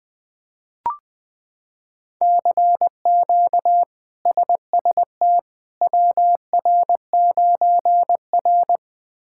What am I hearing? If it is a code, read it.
Morse code, heard as a keyed tone, about 20 wpm, 700 Hz. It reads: CQ SST WR9R
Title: K1USN SST QSO Practice - 20wpm